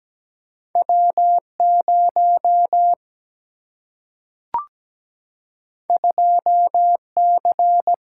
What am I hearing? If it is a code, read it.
Morse code, heard as a keyed tone, about 17 wpm, 700 Hz. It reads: W0 2C
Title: Single-Letter Single-Number 17wpm